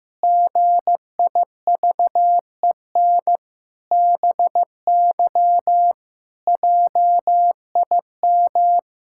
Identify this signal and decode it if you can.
Morse code, heard as a keyed tone, about 15 wpm, 700 Hz. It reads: GIVEN BY JIM
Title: Pangram Challenge v2 - 15wpm